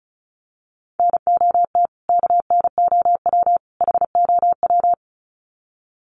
Morse code, heard as a keyed tone, about 34 wpm, 700 Hz. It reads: NOT KNOW HOW
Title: Sets of 3 Words 35wpm